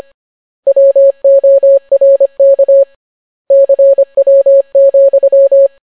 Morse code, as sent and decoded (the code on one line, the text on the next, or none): .-- --- .-. -.- / -.-. .-- --..--
WORK CW,